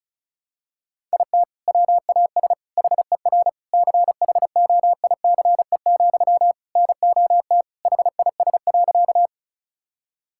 Morse code, heard as a keyed tone, about 34 wpm, 700 Hz. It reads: IT WAS HER CHOICE, NOT HIS.
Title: Sentences from Top 1000 Words - Review Part 6 - 35wpm